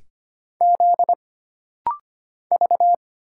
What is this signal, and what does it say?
Morse code, heard as a keyed tone, about 25 wpm, 700 Hz. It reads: Z V
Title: Single Letters - Rapid-Fire 25wpm